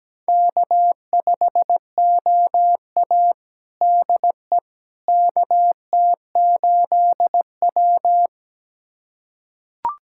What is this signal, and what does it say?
Morse code, heard as a keyed tone, about 17 wpm, 700 Hz. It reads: K5OA DE KT8W
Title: USA Callsign DE Callsign - 4 Characters - 17wpm